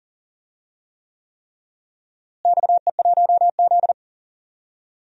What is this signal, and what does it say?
Morse code, heard as a keyed tone, about 37 wpm, 700 Hz. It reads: XE1Z
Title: DXCC-100 Call Signs - 4 Characters - Repeated 3x - 40wpm